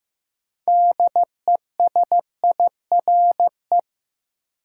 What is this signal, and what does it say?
Morse code, heard as a keyed tone, about 15 wpm, 700 Hz. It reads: DESIRE
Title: Sentences from Top 1100 Words - New Word Drill Alt - 15wpm